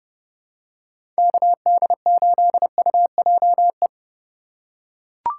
Morse code, heard as a keyed tone, about 30 wpm, 700 Hz. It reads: KD8UJE
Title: USA Call Signs - 3x Character Spacing - 30wpm